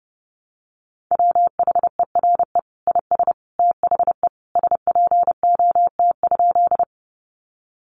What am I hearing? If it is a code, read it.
Morse code, heard as a keyed tone, about 30 wpm, 700 Hz. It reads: WHERE IS THE SPOT?
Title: Sentences from Top 1100 Words 30wpm